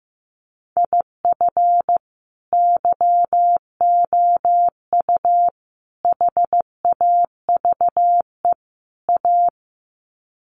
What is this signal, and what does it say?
Morse code, heard as a keyed tone, about 15 wpm, 700 Hz. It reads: IF YOU HAVE A
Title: Sets of 4 Words 15wpm